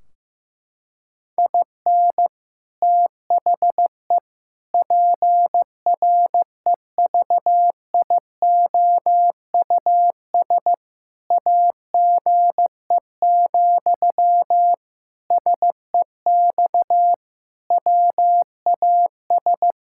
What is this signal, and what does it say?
Morse code, heard as a keyed tone, about 15 wpm, 700 Hz. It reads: IN THE PREVIOUS AGE, SEX WAS
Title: Sentences from Top 1600 Words 15wpm